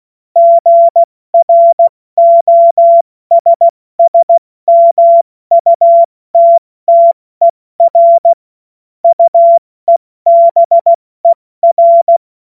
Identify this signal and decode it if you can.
Morse code, heard as a keyed tone, about 16 wpm, 680 Hz. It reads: GROSSMUTTER UEBER
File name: Morsekurs / Witze / Witz 17. August 2021, 70 F80